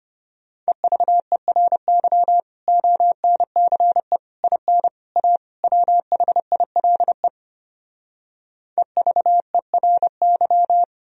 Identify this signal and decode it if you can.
Morse code, heard as a keyed tone, about 33 wpm, 700 Hz. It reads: EVERY ONCE IN A WHILE EVERY